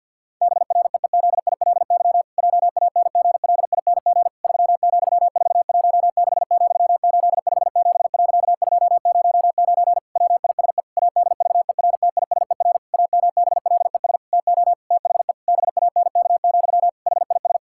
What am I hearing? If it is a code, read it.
Morse code, heard as a keyed tone, about 50 wpm, 700 Hz. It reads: BREEZILY JANGLING 3,416,857.209 WISE ADVERTISER AMBLES TO THE BANK, HIS
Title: Pangram Challenge v2 - 50wpm